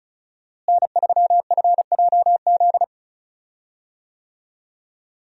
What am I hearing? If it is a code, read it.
Morse code, heard as a keyed tone, about 34 wpm, 700 Hz. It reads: N3FJZ
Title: USA Call Signs - 5 Characters - Repeated 2x - 35wpm